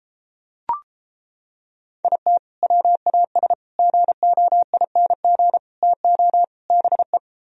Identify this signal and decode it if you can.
Morse code, heard as a keyed tone, about 29 wpm, 700 Hz. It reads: IT WAS GOING TO BE